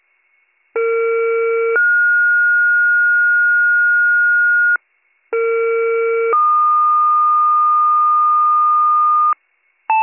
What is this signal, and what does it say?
Two-tone page, heard as a one-second tone, then a three-second tone, about 460 then 1430 Hz, about 460 then 1220 Hz, with noise. It is heard from 0.7 to 9.3 s.